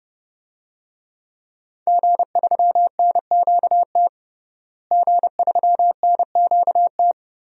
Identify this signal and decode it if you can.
Morse code, heard as a keyed tone, about 30 wpm, 700 Hz. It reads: G3NQT G3NQT
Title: DXCC-100 Call Signs - 5 Characters - Repeated 2x - 30wpm